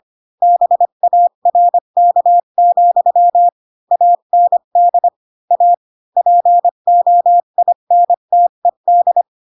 Morse code, heard as a keyed tone, about 25 wpm, 700 Hz. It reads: BARK, AND A POINTED
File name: Pinocchio / Pinocchio ch8 20 25 ik1hge